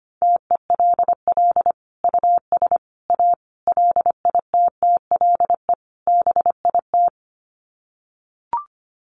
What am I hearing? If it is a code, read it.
Morse code, heard as a keyed tone, about 25 wpm, 700 Hz. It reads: TELL US A LITTLE BIT
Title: Sets of 5 Words Encore2 25wpm